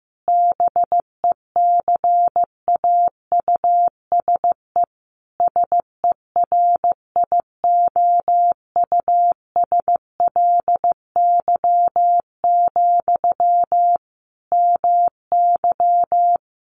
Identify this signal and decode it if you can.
Morse code, heard as a keyed tone, about 15 wpm, 700 Hz. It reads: BECAUSE SERIOUSLY, MY
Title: Sentences from Top 1200 Words 15wpm